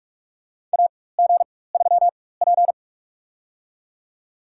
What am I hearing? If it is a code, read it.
Morse code, heard as a keyed tone, about 44 wpm, spaced as for 21 wpm, 700 Hz. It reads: AG3P
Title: USA Call Signs - 4 Characters - 4x Character Spacing - 45wpm